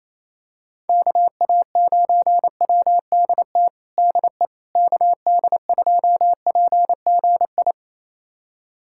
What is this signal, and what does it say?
Morse code, heard as a keyed tone, about 27 wpm, 700 Hz. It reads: KA9WDT DE KD2PGI
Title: USA Callsign DE Callsign - 6 Characters - 28wpm